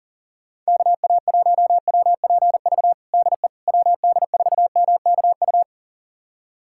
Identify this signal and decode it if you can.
Morse code, heard as a keyed tone, about 40 wpm, 700 Hz. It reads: KA1WPV DE WD4MKU
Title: USA Callsign DE Callsign - 6 Characters - 40wpm